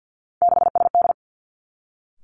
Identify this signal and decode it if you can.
Morse code, heard as a keyed tone, about 50 wpm, 700 Hz. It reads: 6SD
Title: Three Challenging Letter-Number Combinations 50wpm